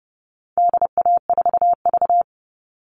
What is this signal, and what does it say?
Morse code, heard as a keyed tone, about 30 wpm, 700 Hz. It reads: DA4V
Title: All Cut Numbers - 4-Digit 30wpm